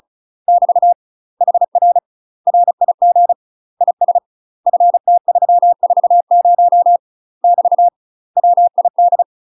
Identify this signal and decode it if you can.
Morse code, heard as a keyed tone, about 36 wpm, 700 Hz. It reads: <BT> HR RIG IS FT340 <BT> WID
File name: UK MASTER QSOs TRAINING / 35WPM / UK MASTER QSOs 0032